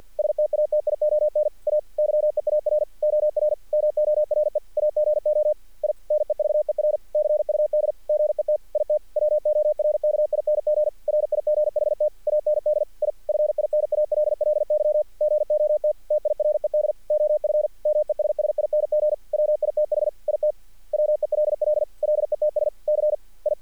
Morse code, heard as a keyed tone, about 50 wpm, 600 Hz. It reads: STATION A YEAR OR MORE AGO I NEVER CUD GET IT WORKING RIGHT AND I FINALLY GOT TIRED OF MESSING WITH IT WELL LETS C I
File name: archived / CRSnet